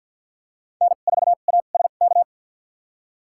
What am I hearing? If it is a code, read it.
Morse code, heard as a keyed tone, about 53 wpm, spaced as for 37 wpm, 700 Hz. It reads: N4ASK
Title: USA Call Signs - 5 Characters - 2x Character Spacing - 50wpm